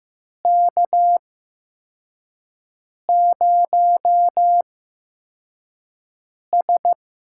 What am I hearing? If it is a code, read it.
Morse code, heard as a keyed tone, about 15 wpm, spaced as for 4 wpm, 700 Hz. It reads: K0S
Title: USA Call Signs - 4 Characters - 8x Character Spacing - 15wpm